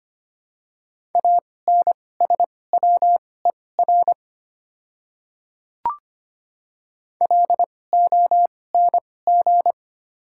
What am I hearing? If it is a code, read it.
Morse code, heard as a keyed tone, about 25 wpm, spaced as for 18 wpm, 690 Hz. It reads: ANSWER LONG
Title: 2x Character Spacing - Top 300 Words - 25wpm